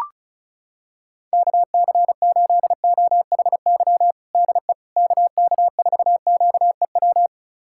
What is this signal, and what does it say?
Morse code, heard as a keyed tone, about 36 wpm, 700 Hz. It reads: KC8OHY DE KK4QEW
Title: USA Callsign DE Callsign 35wpm